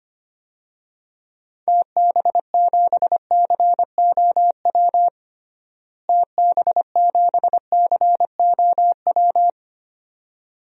Morse code, heard as a keyed tone, about 25 wpm, 700 Hz. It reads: TB7COW TB7COW
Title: DXCC-100 Call Signs - 6 Characters - Repeated 2x - 25wpm